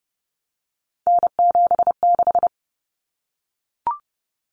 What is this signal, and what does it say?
Morse code, heard as a keyed tone, about 30 wpm, 700 Hz. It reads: N76
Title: Common Cut Numbers - 3-Digit 30wpm